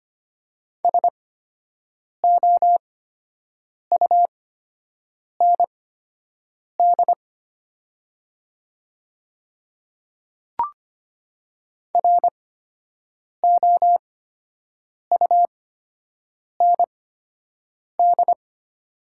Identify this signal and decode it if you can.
Morse code, heard as a keyed tone, about 25 wpm, spaced as for 7 wpm, 700 Hz. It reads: SOUND ROUND
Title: 8x Character Spacing - Top 200 Words - 25wpm